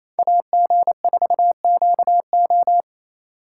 Morse code, heard as a keyed tone, about 28 wpm, 700 Hz. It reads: AG4QO
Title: USA Call Signs - 5 Characters - 4x Character Spacing - 28wpm